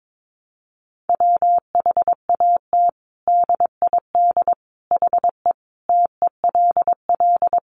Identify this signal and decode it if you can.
Morse code, heard as a keyed tone, about 22 wpm, 700 Hz. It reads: WHAT DID HE TELL
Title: Sentences from Top 200 Words 22wpm